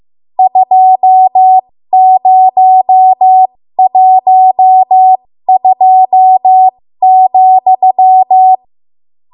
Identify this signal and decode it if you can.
Morse code, heard as a keyed tone, about 15 wpm, spaced as for 13 wpm, 750 Hz. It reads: 2012,
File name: Morse / 13WPM